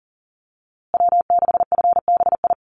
Morse code, heard as a keyed tone, about 38 wpm, 700 Hz. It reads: W6FBI
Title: USA Call Signs Repeated 2x 40wpm